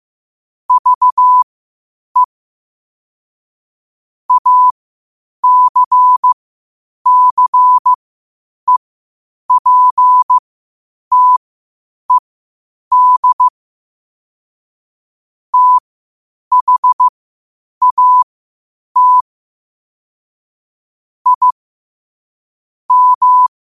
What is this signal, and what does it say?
Morse code, heard as a keyed tone, about 15 wpm, spaced as for 8 wpm, 1000 Hz. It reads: VE ACCEPTED THAT I M
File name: qotdfiles / 8wpm